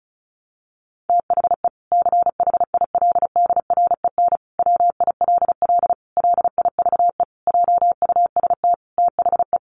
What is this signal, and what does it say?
Morse code, heard as a keyed tone, about 34 wpm, 700 Hz. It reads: THE CHILDREN WILL LIVE JUST THE
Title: Sentences from Top 300 Words 35wpm